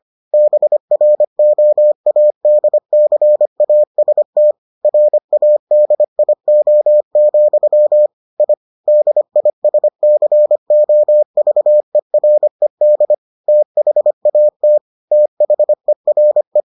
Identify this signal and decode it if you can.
Morse code, heard as a keyed tone, about 25 wpm, 600 Hz. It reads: BROADCAST RADIO, I DISCOVERED THAT THERE